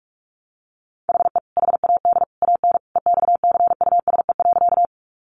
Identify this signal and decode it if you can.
Morse code, heard as a keyed tone, about 44 wpm, 700 Hz. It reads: HE HAD AN EXCUSE.
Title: Sentences from Top 800 Words 45wpm